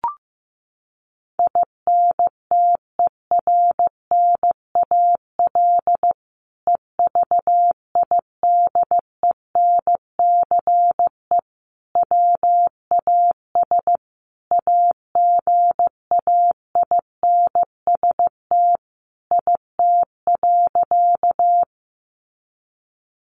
Morse code, heard as a keyed tone, about 15 wpm, 700 Hz. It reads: INTERNAL EVIDENCE WAS AGAINST IT.
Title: Sentences from Top 1400 Words 15wpm